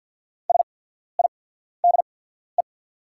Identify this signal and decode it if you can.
Morse code, heard as a keyed tone, about 50 wpm, spaced as for 14 wpm, 690 Hz. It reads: SIDE
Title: 8x Character Spacing - Top 200 Words - 50wpm